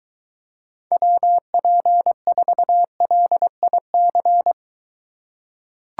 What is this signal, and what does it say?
Morse code, heard as a keyed tone, about 23 wpm, 700 Hz. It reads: WP4LIC